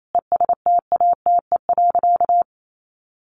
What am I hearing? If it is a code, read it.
Morse code, heard as a keyed tone, about 28 wpm, 700 Hz. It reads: ESTATE.
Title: Sentences from Top 1200 Words 28wpm